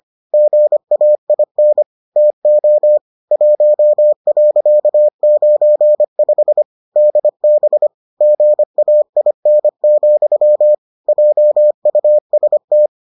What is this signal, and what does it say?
Morse code, heard as a keyed tone, about 25 wpm, 600 Hz. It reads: GAIN TO 1.95 DB GAIN, JUST